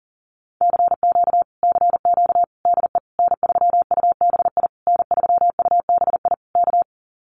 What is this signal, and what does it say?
Morse code, heard as a keyed tone, about 38 wpm, 700 Hz. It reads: CQ CQ DE N3UBI N3UBI K
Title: Common QSO Phrases Encore 40wpm